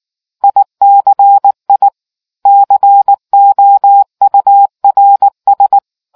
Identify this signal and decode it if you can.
Morse code, heard as a keyed tone, about 19 wpm, 800 Hz. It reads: ICI COURS